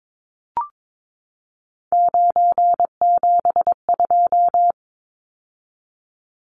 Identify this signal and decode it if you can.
Morse code, heard as a keyed tone, about 22 wpm, 700 Hz. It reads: 972
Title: Numbers - 3-Digit 22wpm